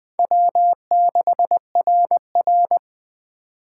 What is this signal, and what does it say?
Morse code, heard as a keyed tone, about 20 wpm, 700 Hz. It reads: W6RR
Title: USA Call Signs - 4 Characters - 8x Character Spacing - 20wpm